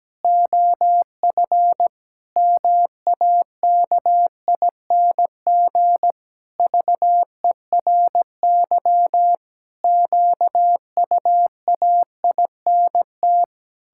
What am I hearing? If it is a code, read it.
Morse code, heard as a keyed tone, about 17 wpm, 700 Hz. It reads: OF MAKING VERY QUAINT
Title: Pangram Challenge v2 - 17wpm